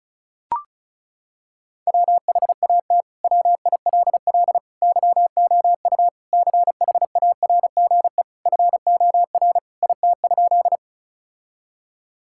Morse code, heard as a keyed tone, about 34 wpm, 700 Hz. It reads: WHAT WILL YOU CHARGE FOR IT?
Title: Sentences from Top 1000 Words - Review Part 6 - 35wpm